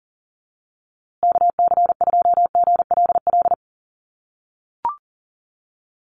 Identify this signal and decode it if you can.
Morse code, heard as a keyed tone, about 40 wpm, 700 Hz. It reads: KC2GLL
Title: USA Call Signs Repeated 3x 40wpm